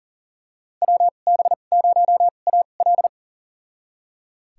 Morse code, heard as a keyed tone, about 40 wpm, spaced as for 29 wpm, 700 Hz. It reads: WB0AL
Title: USA Call Signs - 5 Characters - 2x Character Spacing - 40wpm